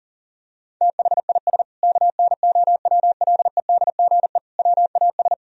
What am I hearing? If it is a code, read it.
Morse code, heard as a keyed tone, about 40 wpm, 700 Hz. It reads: THIS KNOWLEDGE WAS
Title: Sentences from Top 1000 Words - Review Part 3 - 40wpm